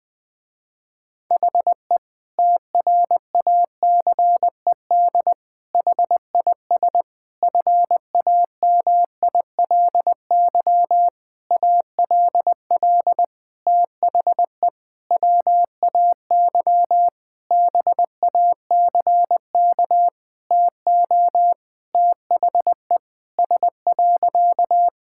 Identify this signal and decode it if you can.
Morse code, heard as a keyed tone, about 20 wpm, 700 Hz. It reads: HE TRACED HIS FAMILY ALL THE WAY BACK TO THE S.
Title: Sentences with 6-Word Phrases - Spoken Beginning - 20wpm